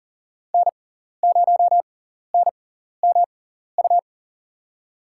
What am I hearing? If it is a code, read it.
Morse code, heard as a keyed tone, about 40 wpm, spaced as for 14 wpm, 700 Hz. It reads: N0NMU
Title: USA Call Signs - 5 Characters - 6x Character Spacing - 40wpm